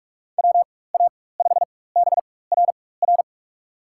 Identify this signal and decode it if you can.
Morse code, heard as a keyed tone, about 47 wpm, spaced as for 21 wpm, 700 Hz. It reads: WA5BRR
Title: USA Call Signs - 6 Characters - 4x Character Spacing - 45wpm